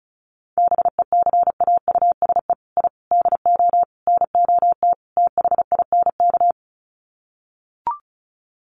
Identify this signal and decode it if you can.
Morse code, heard as a keyed tone, about 34 wpm, 700 Hz. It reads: BECAUSE I DO NOT THINK
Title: Sets of 5 Words Encore 35wpm